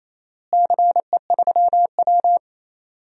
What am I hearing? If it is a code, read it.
Morse code, heard as a keyed tone, about 28 wpm, 700 Hz. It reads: CE3W
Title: DXCC-100 Call Signs - 4 Characters - Repeated 3x - 28wpm